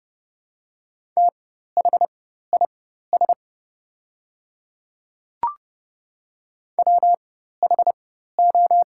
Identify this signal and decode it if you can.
Morse code, heard as a keyed tone, about 29 wpm, spaced as for 14 wpm, 700 Hz. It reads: THIS WHO
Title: 4x Character Spacing - Top 300 Words - 30wpm